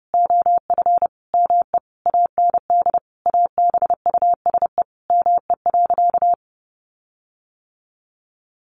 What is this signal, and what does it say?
Morse code, heard as a keyed tone, about 30 wpm, 700 Hz. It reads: OF ME AND ABUSE ME.
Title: Sentences from Top 1300 Words 30wpm